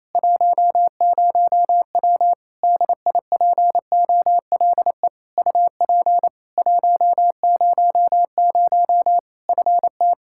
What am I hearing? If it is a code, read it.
Morse code, heard as a keyed tone, about 27 wpm, 700 Hz. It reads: 10W DIPOLE UP 100 FT
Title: Practice QSO - Exchange 2 - Info - 28wpm